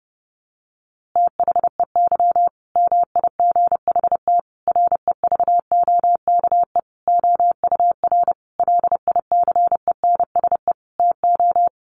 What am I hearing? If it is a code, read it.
Morse code, heard as a keyed tone, about 30 wpm, 700 Hz. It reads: THEY MIGHT REVOKE OUR LICENSE TO
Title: A Sound Of Thunder 30wpm